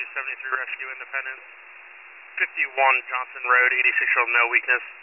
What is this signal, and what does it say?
Emergency Alert System (SAME) tones, heard as harsh buzzing bursts, with speech. It is heard from 1.0 to 3.9 s.
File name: wcalert